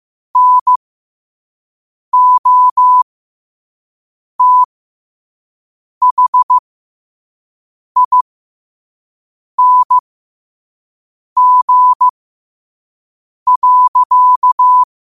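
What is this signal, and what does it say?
Morse code, heard as a keyed tone, about 15 wpm, spaced as for 5 wpm, 1000 Hz. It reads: NOTHING.
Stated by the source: Morse quote of the day at 5 wpm.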